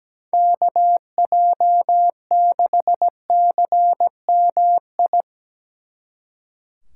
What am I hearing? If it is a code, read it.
Morse code, heard as a keyed tone, about 19 wpm, 700 Hz. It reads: KJ6CMI
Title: Speed Racing - USA Call Signs 17wpm